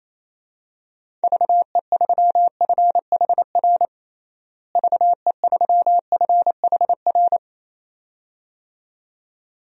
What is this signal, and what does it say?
Morse code, heard as a keyed tone, about 28 wpm, 700 Hz. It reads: VE3FHR VE3FHR
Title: DXCC-100 Call Signs - 6 Characters - Repeated 2x - 28wpm